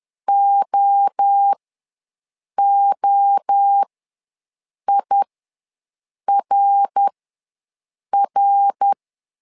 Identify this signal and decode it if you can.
Morse code, heard as a keyed tone, about 10 wpm, spaced as for 6 wpm, 800 Hz. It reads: OOIRR
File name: LEZIONI / 3 MWRO / 3 MWRO ws 10wpm 2310